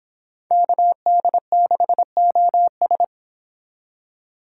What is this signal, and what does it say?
Morse code, heard as a keyed tone, about 30 wpm, 700 Hz. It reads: KD6OS